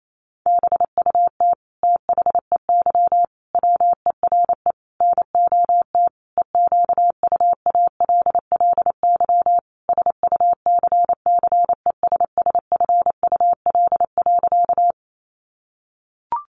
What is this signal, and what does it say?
Morse code, heard as a keyed tone, about 28 wpm, 700 Hz. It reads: BUT THEY WERE NOT EQUALLY SUCCESSFUL.
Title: Sentences from Top 1400 Words 28wpm